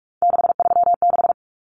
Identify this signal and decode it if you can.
Morse code, heard as a keyed tone, about 44 wpm, 700 Hz. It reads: 636
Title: Numbers - 3-Digit 45wpm